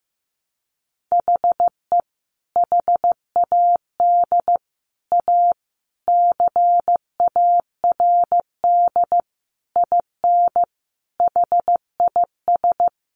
Morse code, heard as a keyed tone, about 15 wpm, 700 Hz. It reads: HE HAD A CARD IN HIS
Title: Sentences from Top 600 Words 15wpm